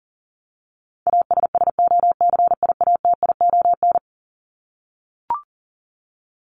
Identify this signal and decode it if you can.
Morse code, heard as a keyed tone, about 40 wpm, 700 Hz. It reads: ASSOCIATION
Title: Top 1001-1250 Words 40wpm